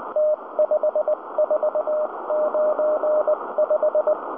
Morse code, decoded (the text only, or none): T5495